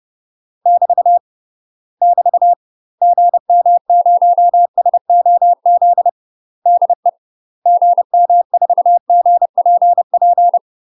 Morse code, heard as a keyed tone, about 30 wpm, 700 Hz. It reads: <BT> <BT> GM0SOZ DE GM4GPP
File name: UK MASTER QSOs TRAINING / 30WPM / UK MASTER QSOs 0050